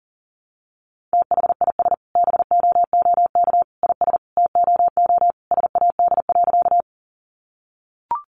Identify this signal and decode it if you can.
Morse code, heard as a keyed tone, about 40 wpm, 700 Hz. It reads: THIS BOOK IS TOO SAD.
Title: Additional Easy Phrases 40wpm